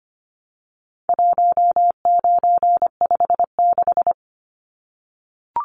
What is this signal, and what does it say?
Morse code, heard as a keyed tone, about 25 wpm, 700 Hz. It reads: 1956
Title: Numbers - 4-Digit 25wpm